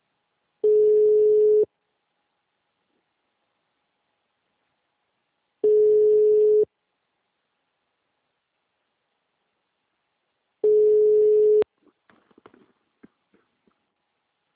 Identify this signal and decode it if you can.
Ringback tone (Europe, 425 Hz).